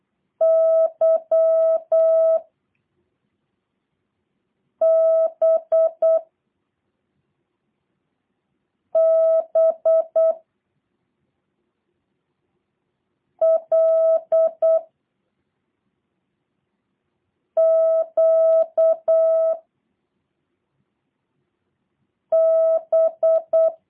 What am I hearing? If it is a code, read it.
Morse code, heard as a keyed tone, about 8 wpm, 650 Hz. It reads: Y B B L Q B
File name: jh8cbh / 2023cw / review1-6